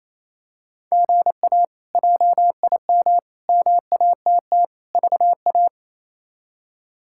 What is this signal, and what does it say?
Morse code, heard as a keyed tone, about 28 wpm, 700 Hz. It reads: GA JIM MATT VA
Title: K1USN SST QSO Practice - 28wpm